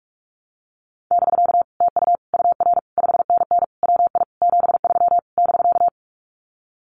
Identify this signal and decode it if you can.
Morse code, heard as a keyed tone, about 44 wpm, 700 Hz. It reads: <BK> TU UR 5NN WI 73 <BK>
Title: Common QSO Phrases Encore 45wpm